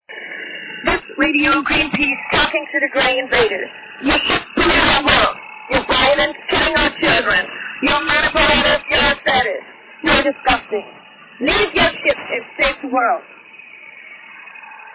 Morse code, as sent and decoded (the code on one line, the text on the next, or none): none